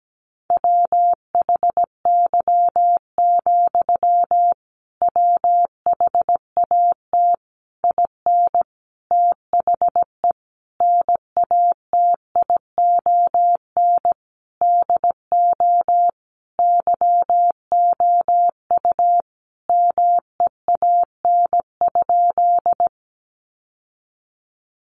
Morse code, heard as a keyed tone, about 17 wpm, 700 Hz. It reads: WHY, WHAT IN THE NATION DO YOU MEAN?
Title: Sentences from Top 900 Words 17wpm